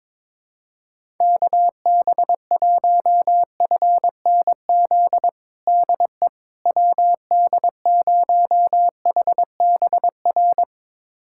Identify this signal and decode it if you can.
Morse code, heard as a keyed tone, about 22 wpm, 700 Hz. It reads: KB1FNZ DE WD0HBR
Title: USA Callsign DE Callsign - 6 Characters - 22wpm